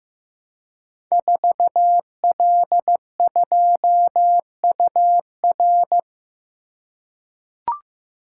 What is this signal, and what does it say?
Morse code, heard as a keyed tone, about 15 wpm, 700 Hz. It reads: 4L2UR
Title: Code Groups - 5 Letter:Numbers - 15wpm